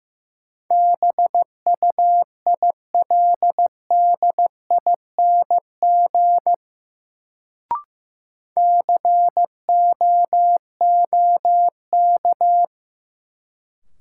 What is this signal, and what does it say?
Morse code, heard as a keyed tone, about 15 wpm, 700 Hz. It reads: BUILDING COOK
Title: Sentences from Top 700 Words - New Word Drill - 15wpm